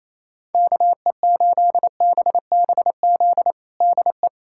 Morse code, heard as a keyed tone, about 28 wpm, 700 Hz. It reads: KE8BBZ DE
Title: USA Callsign DE Callsign - 6 Characters - 28wpm